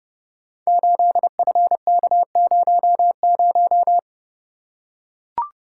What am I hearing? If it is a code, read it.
Morse code, heard as a keyed tone, about 28 wpm, 700 Hz. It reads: 8FK00
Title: Code Groups - 5 Letter:Numbers - 30wpm